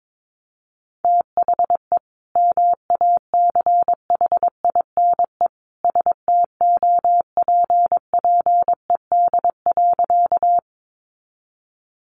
Morse code, heard as a keyed tone, about 22 wpm, 700 Hz. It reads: THE MACHINE STOPPED.
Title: A Sound Of Thunder 22wpm